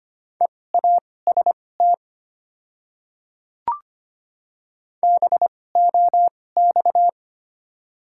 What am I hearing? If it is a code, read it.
Morse code, heard as a keyed tone, about 25 wpm, spaced as for 18 wpm, 700 Hz. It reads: EAST BOX
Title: 2x Character Spacing - Top 500 Words - 25wpm